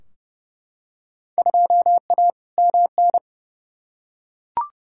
Morse code, heard as a keyed tone, about 31 wpm, 700 Hz. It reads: 2A MN
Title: Field Day Exchanges - 30wpm